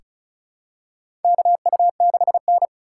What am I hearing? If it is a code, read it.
Morse code, heard as a keyed tone, about 36 wpm, 700 Hz. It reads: KU6N
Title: USA Call Signs - 4 Characters - 4x Character Spacing - 35wpm